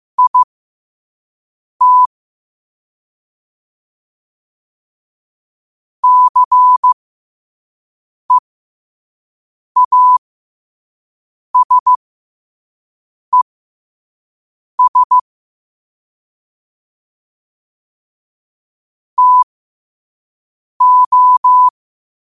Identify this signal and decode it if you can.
Morse code, heard as a keyed tone, about 15 wpm, spaced as for 5 wpm, 1000 Hz. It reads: IT CEASES TO
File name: qotdfiles / 5wpm